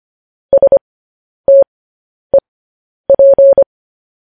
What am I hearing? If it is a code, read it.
Morse code, heard as a keyed tone, about 25 wpm, spaced as for 10 wpm, 550 Hz. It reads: STEP
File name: podcast / News headlines-1x-25x10